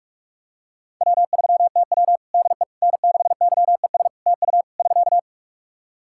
Morse code, heard as a keyed tone, about 46 wpm, 700 Hz. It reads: W3TW DE N6YES TU <SK>
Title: Practice QSO - Exchange 4 - Ending - 45wpm